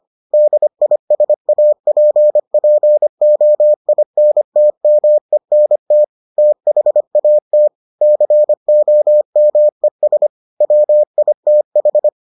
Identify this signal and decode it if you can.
Morse code, heard as a keyed tone, about 25 wpm, 600 Hz. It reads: DISAPPOINTMENT THAT COMES WITH